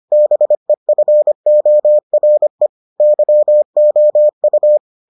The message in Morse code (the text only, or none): BEFORE YOU